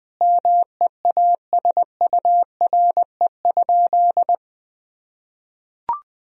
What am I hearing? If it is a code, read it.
Morse code, heard as a keyed tone, about 20 wpm, 700 Hz. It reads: MEASURE?
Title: Sentences from Top 1000 Words - Review Part 2 - 20wpm